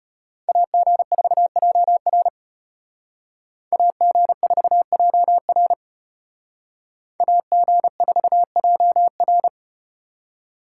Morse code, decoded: AG4JR AG4JR AG4JR

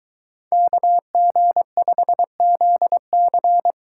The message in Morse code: KG5ZC

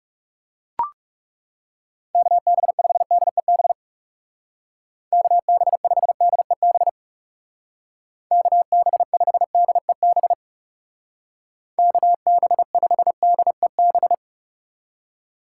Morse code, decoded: KB5DEB KB5DEB KB5DEB KB5DEB

